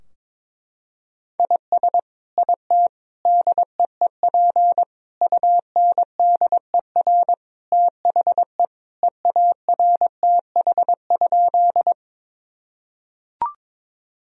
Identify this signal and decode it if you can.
Morse code, heard as a keyed tone, about 22 wpm, 700 Hz. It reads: IS IT DEEP UNDER THE EARTH?
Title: Sentences from Top 1000 Words - Review Part 3 - 22wpm